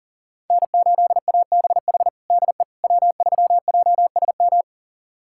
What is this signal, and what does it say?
Morse code, heard as a keyed tone, about 40 wpm, 700 Hz. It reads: N8ABH DE W3JSM
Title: USA Callsign DE Callsign - 5 Characters - 40wpm